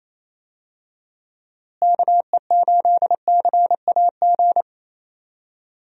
Morse code, heard as a keyed tone, about 28 wpm, 700 Hz. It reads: KE8CAG
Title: USA Call Signs - 6 Characters - Repeated 2x - 28wpm